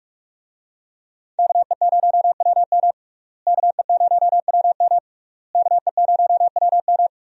Morse code, heard as a keyed tone, about 46 wpm, 700 Hz. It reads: KE0WM KE0WM KE0WM
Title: USA Call Signs - 5 Characters - Repeated 3x - 45wpm